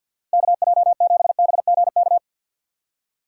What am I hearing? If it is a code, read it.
Morse code, heard as a keyed tone, about 46 wpm, 700 Hz. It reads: KJ7BGK
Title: USA Call Signs - 6 Characters - 6x Character Spacing - 50wpm